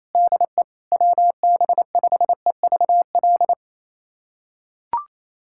Morse code, heard as a keyed tone, about 27 wpm, 700 Hz. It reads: DE WB5EVL
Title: USA Callsign DE Callsign 28wpm